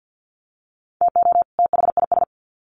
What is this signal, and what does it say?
Morse code, heard as a keyed tone, about 50 wpm, 700 Hz. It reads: TO THIS
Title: Sets of 2 Words 50wpm